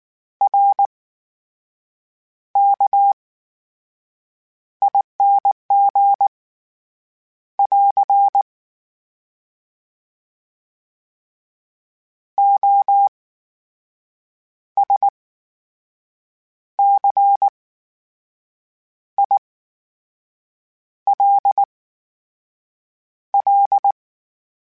Morse code, heard as a keyed tone, about 19 wpm, spaced as for 6 wpm, 800 Hz. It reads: RKING <AR> OSCILL
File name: Special INGs / Special INGs-mp3s / VerySlowToFaster Ing @18